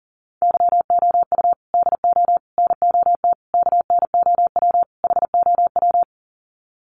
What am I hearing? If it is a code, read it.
Morse code, heard as a keyed tone, about 40 wpm, 700 Hz. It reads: YOU DO NOT KNOW HOW
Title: Sets of 5 Words Encore 40wpm